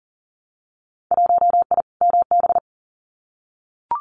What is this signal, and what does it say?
Morse code, heard as a keyed tone, about 43 wpm, 700 Hz. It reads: 1I MB
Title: Winter Field Day Exchanges Encore 40wpm